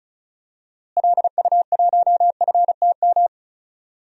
Morse code, heard as a keyed tone, about 34 wpm, 700 Hz. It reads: LU1FTM
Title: DXCC-100 Call Signs - 6 Characters - Repeated 3x - 35wpm